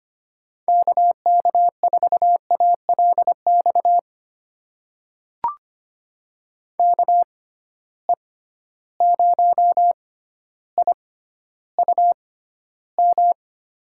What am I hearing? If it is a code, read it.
Morse code, heard as a keyed tone, about 25 wpm, spaced as for 13 wpm, 700 Hz. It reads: KK4ALX KE0IUM
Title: USA Call Signs - 6 Characters - 6x Character Spacing - 25wpm